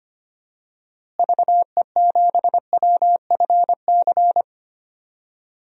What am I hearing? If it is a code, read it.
Morse code, heard as a keyed tone, about 25 wpm, 700 Hz. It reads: VE7WFC
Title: DXCC-100 Call Signs - 6 Characters - Repeated 3x - 25wpm